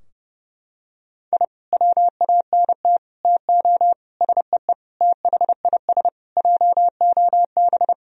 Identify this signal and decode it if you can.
Morse code, heard as a keyed tone, about 32 wpm, 700 Hz. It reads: I WANT TO SEE THIS JOB
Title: Sentences from Top 1000 Words - Review Part 3 - 30wpm